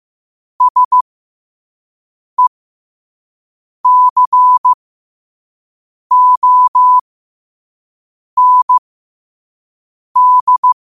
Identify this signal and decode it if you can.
Morse code, heard as a keyed tone, about 15 wpm, spaced as for 5 wpm, 1000 Hz. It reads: SECOND
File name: qotdfiles / 5wpm